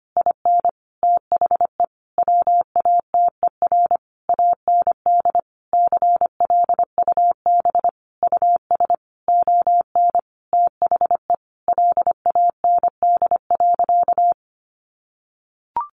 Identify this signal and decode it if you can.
Morse code, heard as a keyed tone, about 25 wpm, 700 Hz. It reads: IN THE WATER AND CLUB US ON THE LAND.
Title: Sentences from Top 1200 Words 25wpm